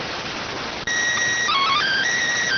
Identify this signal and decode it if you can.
SSTV (slow-scan television), PD 290.